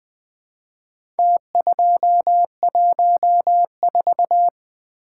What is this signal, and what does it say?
Morse code, heard as a keyed tone, about 20 wpm, 700 Hz. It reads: T214
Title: Four Letter-Number Combinations - 20wpm